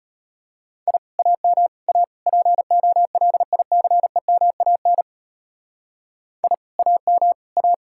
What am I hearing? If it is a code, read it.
Morse code, heard as a keyed tone, about 32 wpm, 700 Hz. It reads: I AM A POLICEMAN I AM A